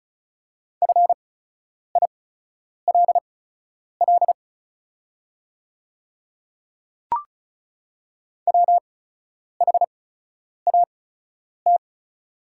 Morse code, heard as a keyed tone, about 34 wpm, spaced as for 10 wpm, 690 Hz. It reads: FILL WHAT
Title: 8x Character Spacing - Top 500 Words - 35wpm